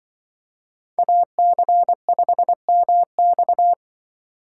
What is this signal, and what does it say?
Morse code, heard as a keyed tone, about 24 wpm, 700 Hz. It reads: AC5MX
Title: Speed Racing - USA 2x2 Call Signs 20wpm